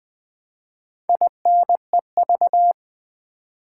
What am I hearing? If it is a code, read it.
Morse code, heard as a keyed tone, about 20 wpm, 700 Hz. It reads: INEV
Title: Build A Word - 4 Letters At A Time - 10-Letter Words - 20wpm